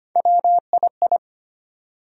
Morse code, heard as a keyed tone, about 25 wpm, 700 Hz. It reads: WII
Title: USA Call Sign Postfixes - 3 Letters 25wpm